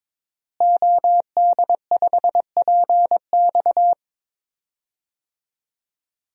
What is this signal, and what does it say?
Morse code, heard as a keyed tone, about 22 wpm, 700 Hz. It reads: OD5PX
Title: DXCC-100 Call Signs - 5 Characters - Repeated 3x - 22wpm